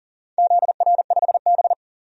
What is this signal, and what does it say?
Morse code, heard as a keyed tone, about 40 wpm, 700 Hz. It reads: ZR5B
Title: DXCC-100 Call Signs - 4 Characters - Repeated 3x - 40wpm